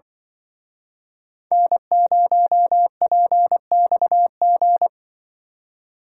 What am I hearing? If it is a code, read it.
Morse code, heard as a keyed tone, about 25 wpm, 700 Hz. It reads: N0PXG
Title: Speed Racing - USA 1x3 Call Signs 20wpm